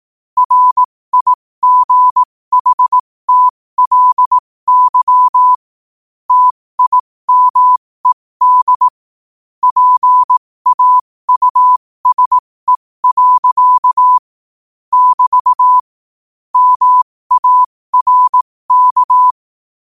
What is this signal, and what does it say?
Morse code, heard as a keyed tone, about 18 wpm, spaced as for 15 wpm, 1000 Hz. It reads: RIGHTLY TIMED PAUSE. <BT> MARK
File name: qotdfiles / 15wpm